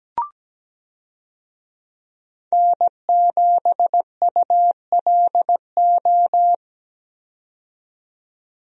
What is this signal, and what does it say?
Morse code, heard as a keyed tone, about 17 wpm, 700 Hz. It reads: N7ULO
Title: USA Call Signs - 5 Characters - 17wpm